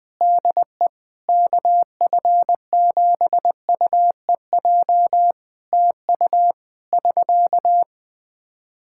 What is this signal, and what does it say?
Morse code, heard as a keyed tone, about 21 wpm, 700 Hz. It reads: DE KF7UEJ TU <SK>
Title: K1USN SST QSO Practice 20wpm